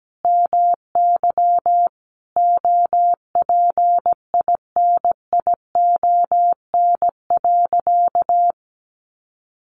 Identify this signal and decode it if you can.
Morse code, heard as a keyed tone, about 17 wpm, 700 Hz. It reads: MY OPINION.
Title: Sentences from Top 1200 Words 17wpm